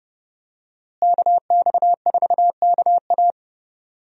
Morse code, heard as a keyed tone, about 30 wpm, 700 Hz. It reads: KX4KA
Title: USA Call Signs - 5 Characters - Repeated 2x - 30wpm